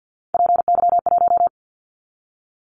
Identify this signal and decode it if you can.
Morse code, heard as a keyed tone, about 50 wpm, 700 Hz. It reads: PY1
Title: Callsign Prefixes for Top 100 DXCC - 50wpm